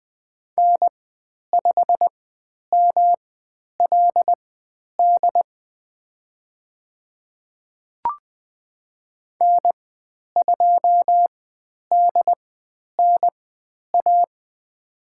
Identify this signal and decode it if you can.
Morse code, heard as a keyed tone, about 20 wpm, spaced as for 10 wpm, 700 Hz. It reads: N5MLD N2DNA
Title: 109 Lesson 18g - Letter m - Call Signs 20wpm